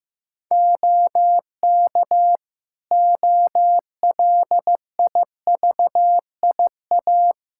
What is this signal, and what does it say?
Morse code, heard as a keyed tone, about 15 wpm, 700 Hz. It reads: OK OLIVIA
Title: Practice QSO - Exchange 2 - Info - 15wpm